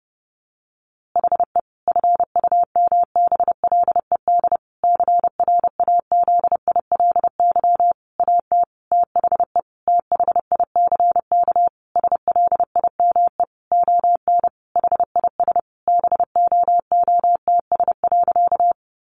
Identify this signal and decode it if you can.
Morse code, heard as a keyed tone, about 30 wpm, 700 Hz. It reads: HE FUMBLED CRAZILY AT THE THICK SLIME ON HIS BOOTS.
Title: A Sound Of Thunder 30wpm